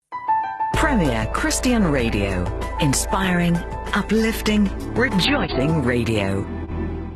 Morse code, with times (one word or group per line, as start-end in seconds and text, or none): none